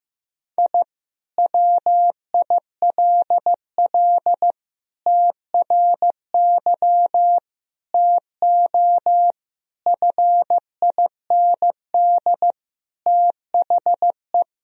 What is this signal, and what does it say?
Morse code, heard as a keyed tone, about 15 wpm, 700 Hz. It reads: I WILL TRY TO FIND THE
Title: Sentences from Top 1000 Words - Review Part 4 - 15wpm